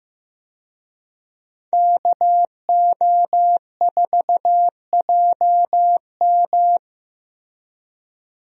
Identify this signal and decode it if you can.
Morse code, heard as a keyed tone, about 15 wpm, 700 Hz. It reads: KO4JM
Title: USA Call Signs - 5 Characters - Repeated 2x - 15wpm